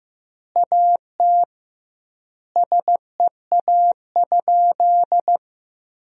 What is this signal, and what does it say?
Morse code, heard as a keyed tone, about 15 wpm, 700 Hz. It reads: AT SEA?
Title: 2x Word-Spacing - Sentences from Top 300 Words - 15wpm